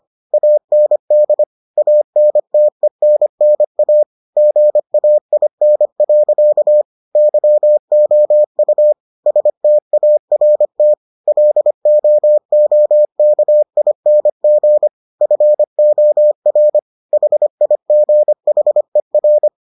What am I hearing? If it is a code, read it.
Morse code, heard as a keyed tone, about 25 wpm, 600 Hz. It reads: AND ANTENNA GAIN. YOU START LOOKING FOR HIGHER